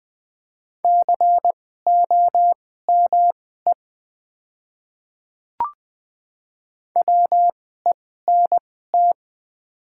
Morse code, heard as a keyed tone, about 20 wpm, spaced as for 14 wpm, 690 Hz. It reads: COME WENT
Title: 2x Character Spacing - Top 500 Words - 20wpm